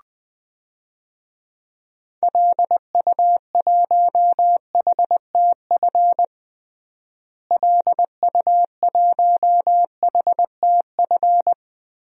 Morse code, heard as a keyed tone, about 20 wpm, 700 Hz. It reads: LU1HTF LU1HTF
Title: DXCC-100 Call Signs - 6 Characters - Repeated 2x - 20wpm